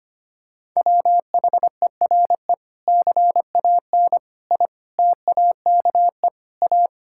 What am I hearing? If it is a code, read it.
Morse code, heard as a keyed tone, about 25 wpm, 700 Hz. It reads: WHERE CAN I TAKE A
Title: Sentences from Top 1600 Words 25wpm